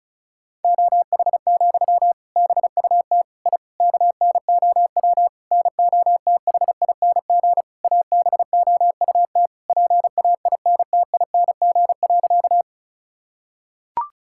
Morse code, heard as a keyed tone, about 34 wpm, 700 Hz. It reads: OH, BUT I KNOW NOTHING ABOUT PAINTING.
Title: Sentences from Top 1000 Words - Review Part 6 - 35wpm